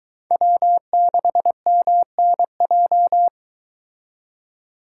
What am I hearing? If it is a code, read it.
Morse code, heard as a keyed tone, about 23 wpm, 700 Hz. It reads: W6MNJ